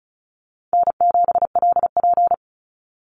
Morse code, heard as a keyed tone, about 34 wpm, 700 Hz. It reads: N7LP
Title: NA SOTA Chaser Callsigns 35wpm